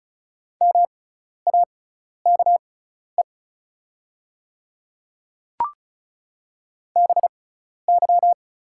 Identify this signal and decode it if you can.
Morse code, heard as a keyed tone, about 36 wpm, spaced as for 12 wpm, 690 Hz. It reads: MAKE BY
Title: 6x Character Spacing - Top 200 Words - 35wpm